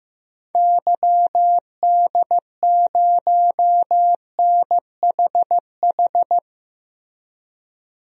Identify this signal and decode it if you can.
Morse code, heard as a keyed tone, about 15 wpm, 700 Hz. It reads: YD0NHH
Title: DXCC-100 Call Signs - 6 Characters - Repeated 2x - 15wpm